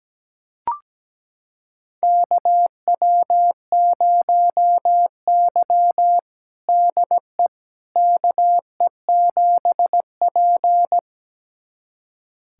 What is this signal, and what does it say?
Morse code, heard as a keyed tone, about 17 wpm, 700 Hz. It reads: KW0Y DE KE7P
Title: USA Callsign DE Callsign - 4 Characters - 17wpm